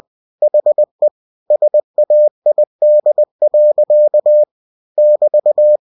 Morse code, heard as a keyed tone, about 20 wpm, 600 Hz. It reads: HE SAID. <BT>
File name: cw-books / winnie-the-pooh / winnie-20wpm-page-0048